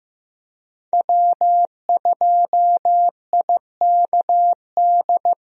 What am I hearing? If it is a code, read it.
Morse code, heard as a keyed tone, about 15 wpm, 700 Hz. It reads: W2IKD
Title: USA Call Signs - 5 Characters - 15wpm